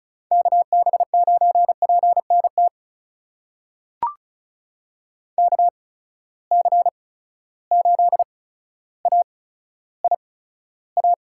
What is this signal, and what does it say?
Morse code, heard as a keyed tone, about 36 wpm, spaced as for 15 wpm, 700 Hz. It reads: KB9PNT KC8AIA
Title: USA Call Signs - 6 Characters - 8x Character Spacing - 35wpm